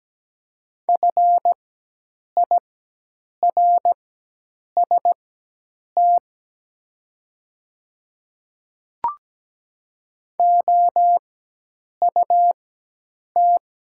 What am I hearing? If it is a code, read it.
Morse code, heard as a keyed tone, about 17 wpm, spaced as for 8 wpm, 700 Hz. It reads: FIRST OUT
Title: 4x Character Spacing - Top 100 Words - 17wpm